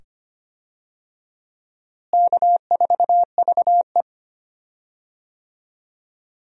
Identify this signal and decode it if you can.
Morse code, heard as a keyed tone, about 25 wpm, 700 Hz. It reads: K4VE
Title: USA Call Signs - 4 Characters - Repeated 2x - 25wpm